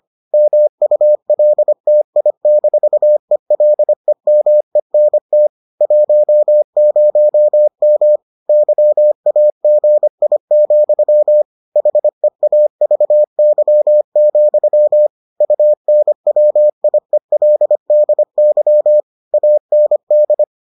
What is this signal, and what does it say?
Morse code, heard as a keyed tone, about 25 wpm, 600 Hz. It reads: MULTI-ELEMENT 10M YAGI, HEAVY, UNWIELDY AND